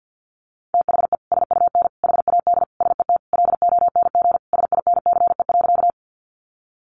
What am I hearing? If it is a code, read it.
Morse code, heard as a keyed tone, about 50 wpm, 700 Hz. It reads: THE SUN HAD SET LONG SINCE.
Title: Sentences from Top 300 Words 50wpm